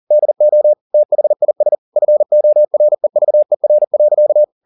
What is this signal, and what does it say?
Morse code, heard as a keyed tone, about 40 wpm, 600 Hz. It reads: DO THIS FOREVER.